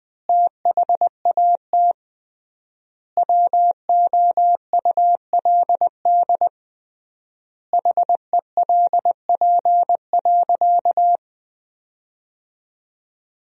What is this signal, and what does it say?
Morse code, heard as a keyed tone, about 21 wpm, 700 Hz. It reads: THAT WOULD HELP.
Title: 3x Word-Spacing - Sentences from Top 200 Words - 20wpm